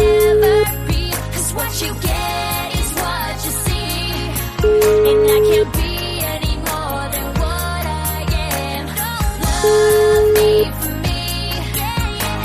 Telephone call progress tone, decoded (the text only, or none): ringback tone